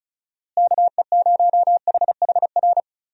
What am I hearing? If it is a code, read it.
Morse code, heard as a keyed tone, about 36 wpm, 700 Hz. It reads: KE0HHR
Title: USA Call Signs - 6 Characters - Repeated 3x - 35wpm